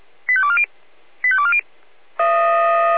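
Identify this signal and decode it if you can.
Five-tone selective call (ZVEI), heard as quick runs of five tones, tones of about 70 ms. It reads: 85240 85240
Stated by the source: ZVEI five-tone selective call received on an SDR.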